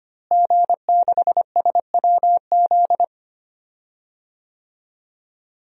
Morse code, heard as a keyed tone, about 25 wpm, 700 Hz. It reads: G6SWZ
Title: DXCC-100 Call Signs - 5 Characters - Repeated 2x - 25wpm